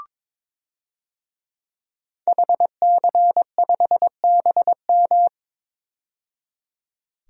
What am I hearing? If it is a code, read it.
Morse code, heard as a keyed tone, about 22 wpm, 700 Hz. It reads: HC5BM
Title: DXCC-100 Call Signs - 5 Characters - 22wpm